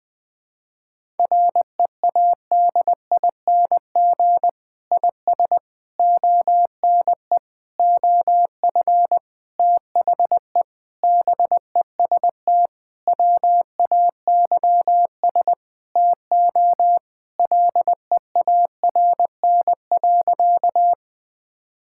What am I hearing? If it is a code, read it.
Morse code, heard as a keyed tone, about 20 wpm, 700 Hz. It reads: READING IS ONE OF THE BEST WAYS TO LEARN.
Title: Sentences with 6-Word Phrases - 20wpm